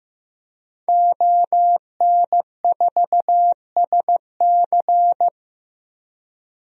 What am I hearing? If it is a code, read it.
Morse code, heard as a keyed tone, about 15 wpm, 700 Hz. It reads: ON4SC
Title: DXCC-100 Call Signs - 5 Characters - 15wpm